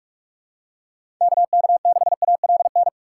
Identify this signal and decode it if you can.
Morse code, heard as a keyed tone, about 46 wpm, 700 Hz. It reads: KK6ALN
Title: USA Call Signs - 6 Characters - 6x Character Spacing - 45wpm